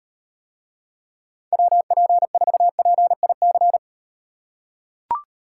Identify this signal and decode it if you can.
Morse code, heard as a keyed tone, about 38 wpm, 700 Hz. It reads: WP4PIC